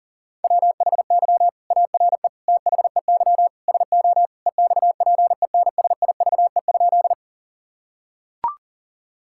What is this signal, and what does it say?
Morse code, heard as a keyed tone, about 40 wpm, 700 Hz. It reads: WHY ARE THEY SO EXPENSIVE?
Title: Sentences from Top 1000 Words - Review Part 6 - 40wpm